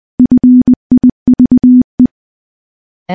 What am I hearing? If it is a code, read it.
Morse code, heard as a keyed tone, about 20 wpm, 250 Hz. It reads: FIVE
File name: morse-training-20WPM